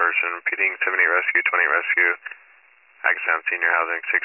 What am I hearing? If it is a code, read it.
Emergency Alert System (SAME) tones, heard as harsh buzzing bursts, with speech. It is heard from 0.4 to 3.4 s.